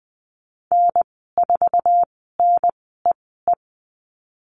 Morse code, heard as a keyed tone, about 19 wpm, spaced as for 14 wpm, 700 Hz. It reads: N4NEE
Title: 055 Lesson 10g - Letter r - Call Signs 20wpm